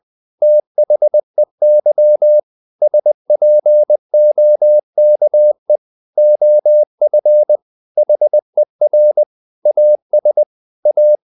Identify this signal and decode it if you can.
Morse code, heard as a keyed tone, about 20 wpm, 600 Hz. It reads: THEY SPOKE OF HER AS A
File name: the old man and the sea / the old man and the sea part 1 20wpm